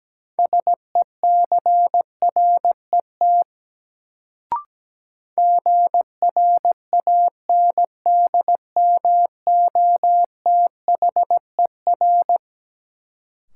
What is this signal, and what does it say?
Morse code, heard as a keyed tone, about 17 wpm, 700 Hz. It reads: SECRET GRANDMOTHER
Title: Sentences from Top 1200 Words - New Word Drill Alt - 17wpm